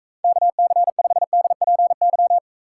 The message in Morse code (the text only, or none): KC5DPY